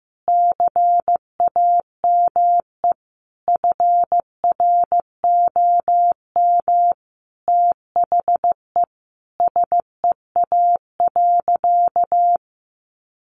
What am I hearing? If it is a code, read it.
Morse code, heard as a keyed tone, about 15 wpm, 700 Hz. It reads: CAME FROM THE SEA.
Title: Sentences from Top 1400 Words 15wpm